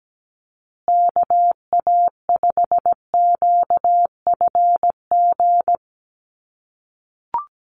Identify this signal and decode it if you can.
Morse code, heard as a keyed tone, about 17 wpm, 700 Hz. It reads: KA5QFG
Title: USA Call Signs Repeated 3x 17wpm